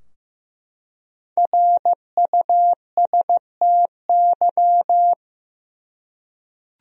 Morse code, heard as a keyed tone, about 15 wpm, 700 Hz. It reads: RUSTY
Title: USA Common Names - Max 5 Letters - 15wpm